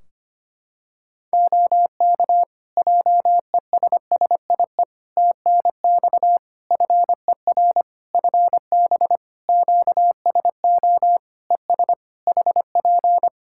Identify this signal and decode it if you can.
Morse code, heard as a keyed tone, about 25 wpm, 700 Hz. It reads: OK JESSIE TNX FER FB QSO ES HP
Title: Practice QSO - Exchange 4 - Ending - 25wpm